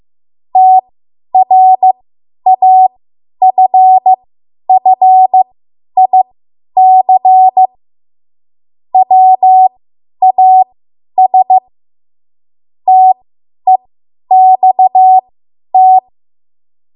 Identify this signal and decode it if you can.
Morse code, heard as a keyed tone, about 15 wpm, spaced as for 10 wpm, 750 Hz. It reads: TRAFFIC WAS TEXT
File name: Morse / 10WPM